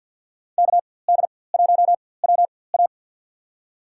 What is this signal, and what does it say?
Morse code, heard as a keyed tone, about 53 wpm, spaced as for 24 wpm, 700 Hz. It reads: KD1WA
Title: USA Call Signs - 4x Character Spacing - 50wpm